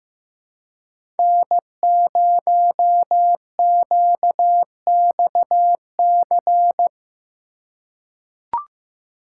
Morse code, heard as a keyed tone, about 15 wpm, 700 Hz. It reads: N0QXC
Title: USA Call Signs - 5 Characters - 2x Character Spacing - 15wpm